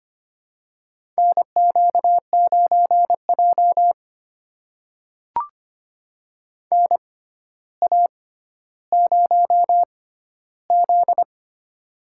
Morse code, heard as a keyed tone, about 25 wpm, spaced as for 13 wpm, 700 Hz. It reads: NQ9J NA0Z
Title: USA Call Signs - 4 Characters - 6x Character Spacing - 25wpm